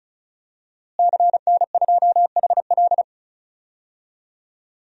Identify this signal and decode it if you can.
Morse code, heard as a keyed tone, about 34 wpm, 700 Hz. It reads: CN2HL
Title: DXCC-100 Call Signs - 5 Characters - Repeated 3x - 35wpm